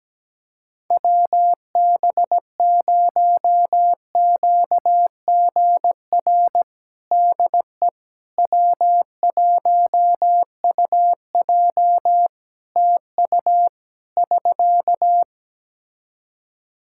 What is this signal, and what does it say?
Morse code, heard as a keyed tone, about 17 wpm, 700 Hz. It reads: WB0QGR DE W1UJ TU <SK>
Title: Practice QSO - Exchange 4 - Ending - 17wpm